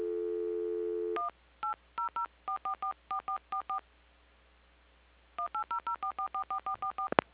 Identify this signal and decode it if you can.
DTMF, heard as telephone touch tones, tones of about 110 ms. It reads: 1 8004444444 18004444444